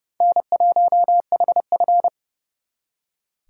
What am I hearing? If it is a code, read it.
Morse code, heard as a keyed tone, about 34 wpm, 700 Hz. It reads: N1HF
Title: Speed Racing - USA 1x2 Call Signs 30wpm